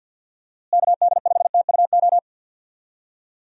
Morse code, heard as a keyed tone, about 51 wpm, 700 Hz. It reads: KD5TUO
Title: USA Call Signs - 6 Characters - Repeated 2x - 50wpm